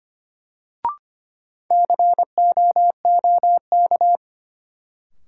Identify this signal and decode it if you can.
Morse code, heard as a keyed tone, about 25 wpm, 700 Hz. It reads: COOK
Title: Sentences from Top 700 Words - New Word Drill - 25wpm